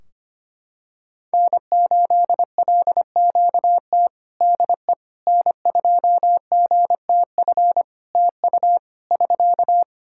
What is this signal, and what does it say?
Morse code, heard as a keyed tone, about 25 wpm, 700 Hz. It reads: N8LQT DE N2GTF TU <SK>
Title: K1USN SST QSO Practice 25wpm